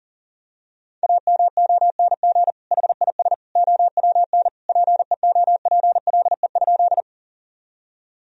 Morse code, heard as a keyed tone, about 40 wpm, 700 Hz. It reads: AMONG HIS OWN PEOPLE?
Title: Sentences from Top 1000 Words - Review Part 3 - 40wpm